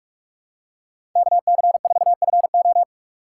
Morse code, heard as a keyed tone, about 42 wpm, 700 Hz. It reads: KC4FO
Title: USA Call Signs - 6x Character Spacing - 45wpm